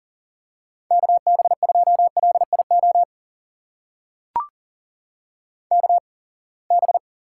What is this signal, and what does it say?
Morse code, heard as a keyed tone, about 42 wpm, spaced as for 28 wpm, 700 Hz. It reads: KB2LIO KB
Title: USA Call Signs - 6 Characters - 8x Character Spacing - 40wpm